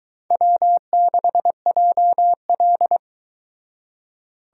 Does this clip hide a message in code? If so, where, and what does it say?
Yes: Morse code from 0.3 to 3.0 s, heard as a keyed tone, about 21 wpm, 700 Hz. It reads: W6JL